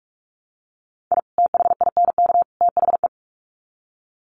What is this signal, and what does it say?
Morse code, heard as a keyed tone, about 44 wpm, 700 Hz. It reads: I THINK THE
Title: Sets of 3 Words 45wpm